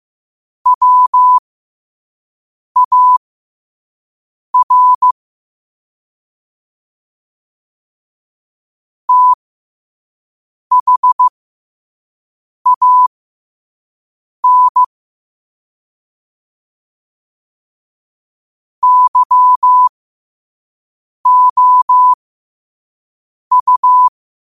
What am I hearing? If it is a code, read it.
Morse code, heard as a keyed tone, about 15 wpm, spaced as for 5 wpm, 1000 Hz. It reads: WAR THAN YOU
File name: 5wpm